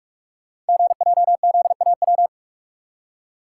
Morse code, heard as a keyed tone, about 44 wpm, 700 Hz. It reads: GJZAW
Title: Code Groups - 5 Letters - 45wpm